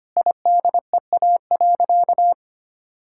Morse code, heard as a keyed tone, about 25 wpm, 700 Hz. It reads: IDEA.
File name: practice / 25wpm / Sentences with 4-Word Phrases - 25wpm